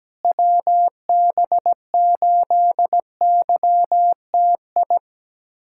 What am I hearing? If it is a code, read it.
Morse code, heard as a keyed tone, about 17 wpm, 700 Hz. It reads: WB8YTI
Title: USA Callsign DE Callsign - 6 Characters - 17wpm